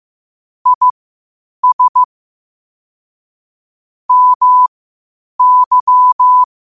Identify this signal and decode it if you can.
Morse code, heard as a keyed tone, about 15 wpm, spaced as for 8 wpm, 1000 Hz. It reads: IS MY
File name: qotdfiles / 8wpm